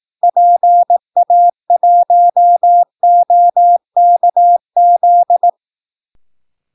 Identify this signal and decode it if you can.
Morse code, heard as a keyed tone, about 18 wpm, 700 Hz. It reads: PA1OKZ
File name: PI4AA Uitzending